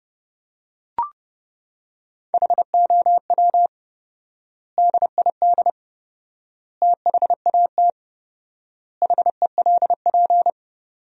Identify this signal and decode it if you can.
Morse code, heard as a keyed tone, about 30 wpm, 700 Hz. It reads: HOW DID THAT HELP
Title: 4x Word-Spacing - Sentences from Top 200 Words - 30wpm